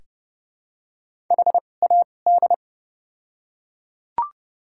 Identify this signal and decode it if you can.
Morse code, heard as a keyed tone, about 30 wpm, spaced as for 22 wpm, 700 Hz. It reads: HAD
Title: 2x Character Spacing - Top 100 Words - 30wpm